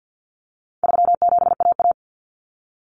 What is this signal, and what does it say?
Morse code, heard as a keyed tone, about 50 wpm, 700 Hz. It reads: 37AA
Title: Common Cut Numbers - 4-Digit 50wpm